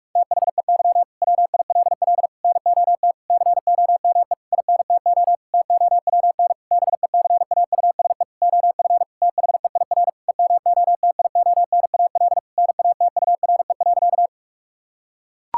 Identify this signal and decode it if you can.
Morse code, heard as a keyed tone, about 41 wpm, 700 Hz. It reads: THEY WILL NOT COME INTO TOWN BECAUSE OF THEIR EMOTIONAL NATURE.
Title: Sentences from Top 1000 Words - Review Part 5 - 45wpm